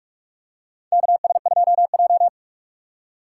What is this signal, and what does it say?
Morse code, heard as a keyed tone, about 41 wpm, 700 Hz. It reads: KS2J
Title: USA Call Signs - 4 Characters - 3x Character Spacing - 45wpm